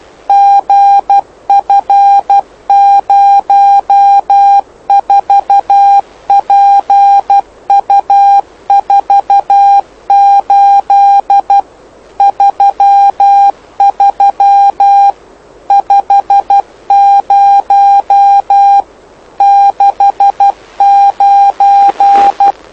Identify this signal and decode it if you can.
Morse code, heard as a keyed tone, about 12 wpm, 800 Hz. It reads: GF04PU48 33 50 69
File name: lu4aao / lu7aa 09dic2011 exp planeador vuelo1